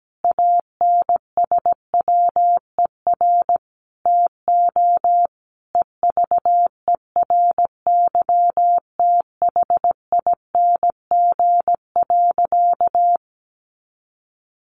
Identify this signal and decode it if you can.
Morse code, heard as a keyed tone, about 17 wpm, 700 Hz. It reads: ANSWER TO EVERYTHING.
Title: Sentences from Top 700 Words 17wpm